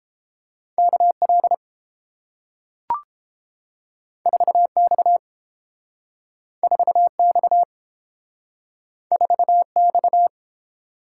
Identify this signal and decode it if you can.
Morse code, heard as a keyed tone, about 31 wpm, 700 Hz. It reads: KL 4X 4X 4X